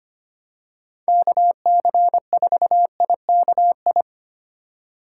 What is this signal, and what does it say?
Morse code, heard as a keyed tone, about 25 wpm, 700 Hz. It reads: KC4IKI
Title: USA Call Signs - 6 Characters - 4x Character Spacing - 25wpm